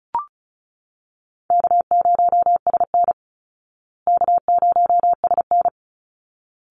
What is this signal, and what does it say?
Morse code, heard as a keyed tone, about 34 wpm, 700 Hz. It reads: K0SN K0SN
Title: Easy Call Signs Repeated 2x - 35wpm